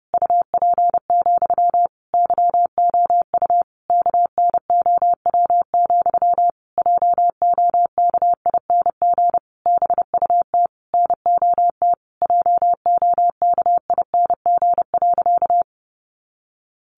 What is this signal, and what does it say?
Morse code, heard as a keyed tone, about 28 wpm, 700 Hz. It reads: UP, YOU KNOW, JOKING BUT NOT JOKING.
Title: A Sound Of Thunder 30wpm